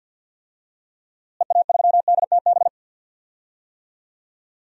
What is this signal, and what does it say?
Morse code, heard as a keyed tone, about 47 wpm, 700 Hz. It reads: EA3DTB
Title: DXCC-100 Call Signs - 6 Characters - Repeated 2x - 50wpm